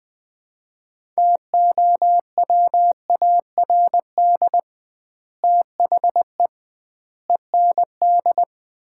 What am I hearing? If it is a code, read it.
Morse code, heard as a keyed tone, about 20 wpm, 700 Hz. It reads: TOWARD THE END
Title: 2x Word-Spacing - Sets of 5 Words - 20wpm